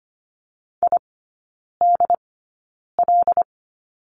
Morse code, heard as a keyed tone, about 25 wpm, spaced as for 9 wpm, 700 Hz. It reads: IDL
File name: Lesson 15e - Number 5 - Three Characters 25wpm